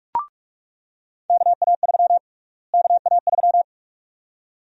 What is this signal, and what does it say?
Morse code, heard as a keyed tone, about 43 wpm, 700 Hz. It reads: KA3 KA3
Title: USA Call Sign Prefixes - Repeated 2x - 45wpm